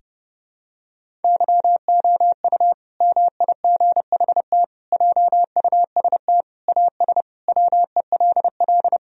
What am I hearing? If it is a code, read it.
Morse code, heard as a keyed tone, about 30 wpm, 700 Hz. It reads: YOU MIGHT JUST AS WELL
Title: Sentences from Top 1000 Words - Review Part 2 - 30wpm